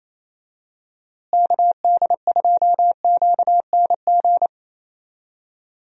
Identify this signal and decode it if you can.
Morse code, heard as a keyed tone, about 28 wpm, 700 Hz. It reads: KD2QNG
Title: USA Call Signs - 6 Characters - Repeated 3x - 28wpm